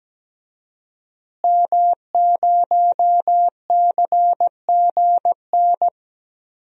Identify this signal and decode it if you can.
Morse code, heard as a keyed tone, about 17 wpm, 700 Hz. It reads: M0CGN
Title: DXCC-100 Call Signs - 5 Characters - Repeated 3x - 17wpm